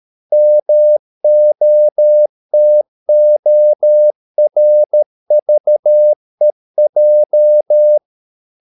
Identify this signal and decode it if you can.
Morse code, heard as a keyed tone, about 13 wpm, 600 Hz. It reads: MOTORVEJ
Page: Ferie 13 wpm